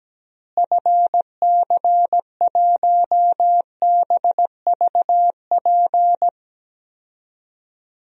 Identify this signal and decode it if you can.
Morse code, heard as a keyed tone, about 17 wpm, 700 Hz. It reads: FC1BVP
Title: DXCC-100 Call Signs - 6 Characters - Repeated 3x - 17wpm